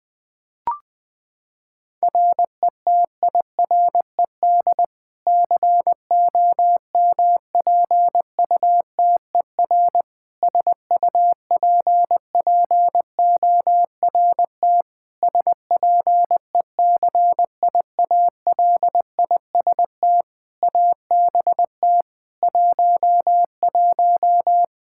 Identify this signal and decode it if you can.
Morse code, heard as a keyed tone, about 21 wpm, 700 Hz. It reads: RETIRED COMPUTER SUPPORT SPECIALIST ABT 11
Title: Practice QSO - Exchange 3 - More Info - 20wpm